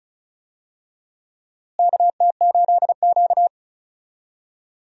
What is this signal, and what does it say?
Morse code, heard as a keyed tone, about 36 wpm, 700 Hz. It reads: KT8Q
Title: USA Call Signs - 4 Characters - Repeated 2x - 35wpm